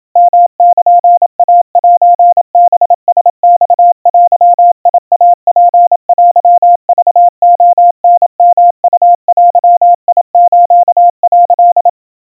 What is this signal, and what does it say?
Morse code (Japanese Wabun code), heard as a keyed tone, about 28 wpm, 700 Hz. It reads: ヨルイセハラマデイツテクレタヨウデス」
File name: kawa / Morse sample / kamyu 100 cpm 700 Hz Crispy